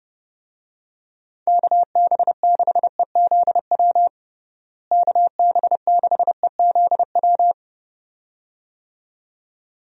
Morse code, heard as a keyed tone, about 30 wpm, 700 Hz. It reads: KB6EZW KB6EZW
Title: USA Call Signs - 6 Characters - Repeated 2x - 30wpm